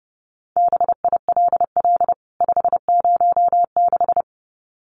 Morse code, heard as a keyed tone, about 30 wpm, 700 Hz. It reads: BILL 506
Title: CWT Exchange 30wpm